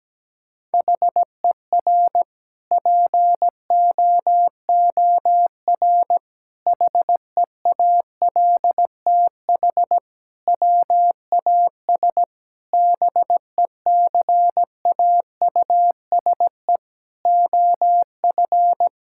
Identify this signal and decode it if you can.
Morse code, heard as a keyed tone, about 17 wpm, 700 Hz. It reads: HER POOR HEALTH WAS BECAUSE OF
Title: Sentences from Top 1000 Words - Review Part 6 - 17wpm